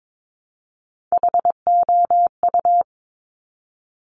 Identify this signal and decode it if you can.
Morse code, heard as a keyed tone, about 22 wpm, 700 Hz. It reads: HOU
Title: Top Three Letter Combinations in Words 22wpm